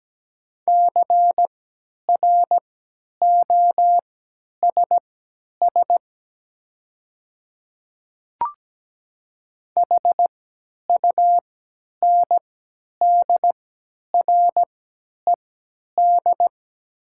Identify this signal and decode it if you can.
Morse code, heard as a keyed tone, about 17 wpm, spaced as for 10 wpm, 700 Hz. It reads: CROSS HUNDRED
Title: 3x Character Spacing - Top 500 Words - 17wpm